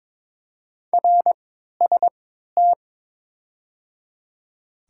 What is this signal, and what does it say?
Morse code, heard as a keyed tone, about 22 wpm, spaced as for 13 wpm, 690 Hz. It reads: RST
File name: Top 90 QSO Elements - 3x Character Spacing - 22wpm